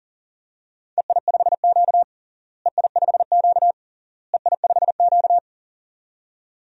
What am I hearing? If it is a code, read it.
Morse code, heard as a keyed tone, about 40 wpm, 700 Hz. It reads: EI5Q EI5Q EI5Q
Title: DXCC-100 Call Signs - 4 Characters - Repeated 3x - 40wpm